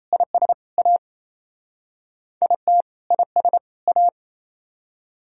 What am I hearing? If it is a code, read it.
Morse code, heard as a keyed tone, about 35 wpm, 700 Hz. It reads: IS A IT IS A